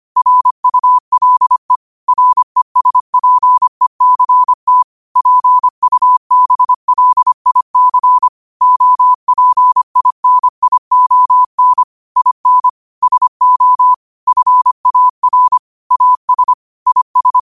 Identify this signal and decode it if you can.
Morse code, heard as a keyed tone, about 25 wpm, 1000 Hz. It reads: RULE RESPECT PUBLIC OPINION IN SO FAR AS IS
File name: qotdfiles / 25wpm